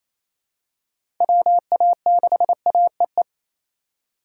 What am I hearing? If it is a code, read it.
Morse code, heard as a keyed tone, about 28 wpm, 700 Hz. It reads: WA6AEE
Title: K1USN SST QSO Practice - 28wpm